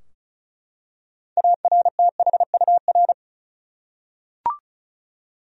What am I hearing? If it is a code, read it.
Morse code, heard as a keyed tone, about 36 wpm, 700 Hz. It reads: ARTHUR
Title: USA Common Names - 35wpm